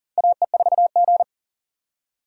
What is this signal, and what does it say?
Morse code, heard as a keyed tone, about 38 wpm, 700 Hz. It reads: AE4G
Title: USA Call Signs - 4 Characters - 2x Character Spacing - 40wpm